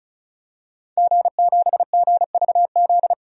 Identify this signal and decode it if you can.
Morse code, heard as a keyed tone, about 36 wpm, 700 Hz. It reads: G7GVZ
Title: DXCC-100 Call Signs - 5 Characters - 35wpm